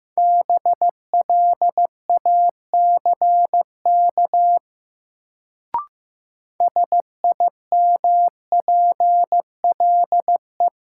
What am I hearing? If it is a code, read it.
Morse code, heard as a keyed tone, about 15 wpm, 700 Hz. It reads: BLACK SIMPLE
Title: Sentences from Top 400 Words - New Word Drill - 15wpm